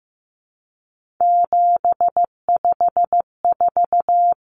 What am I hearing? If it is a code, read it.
Morse code, heard as a keyed tone, about 15 wpm, 700 Hz. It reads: 754
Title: Common Cut Numbers - 3-Digit 15wpm